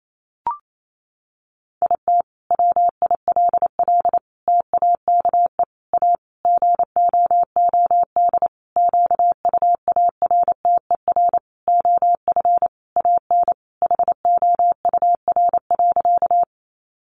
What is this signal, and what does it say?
Morse code, heard as a keyed tone, about 28 wpm, 700 Hz. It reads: IT WILL TAKE A GOOD QUARTER OF AN HOUR.
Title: Sentences from Top 1300 Words 28wpm